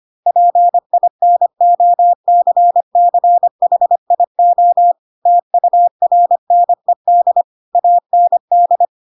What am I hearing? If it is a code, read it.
Morse code, heard as a keyed tone, about 25 wpm, 700 Hz. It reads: PINOCCHIO TURNED AND
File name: Pinocchio / Pinocchio ch4 25 25 ik1hge